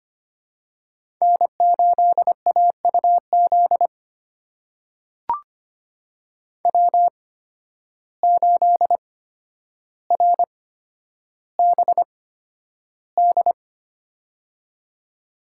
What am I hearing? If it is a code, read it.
Morse code, heard as a keyed tone, about 25 wpm, spaced as for 11 wpm, 700 Hz. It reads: N8AUZ W8RBD
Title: USA Call Signs - 8x Character Spacing - 25wpm